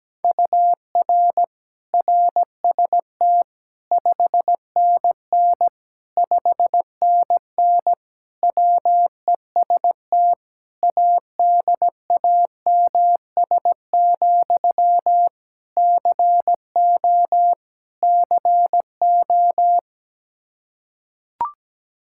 Morse code, heard as a keyed tone, about 17 wpm, 700 Hz. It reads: UR RST 5NN 5NN WEST ADAMS, CO CO
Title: Practice QSO - Exchange 1 - Beginning - 17wpm